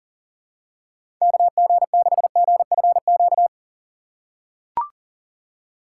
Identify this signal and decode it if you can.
Morse code, heard as a keyed tone, about 40 wpm, 700 Hz. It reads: KG6GFQ
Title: USA Call Signs - 6 Characters - 3x Character Spacing - 40wpm